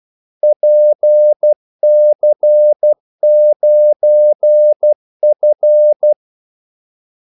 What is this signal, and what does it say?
Morse code, heard as a keyed tone, about 12 wpm, 600 Hz. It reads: PC9F
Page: 2 ekstra ordmellemrum mellem kaldesignalerne